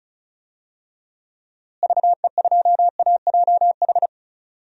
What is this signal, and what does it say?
Morse code, heard as a keyed tone, about 34 wpm, 700 Hz. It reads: VE2AJH
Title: DXCC-100 Call Signs - 6 Characters - Repeated 2x - 35wpm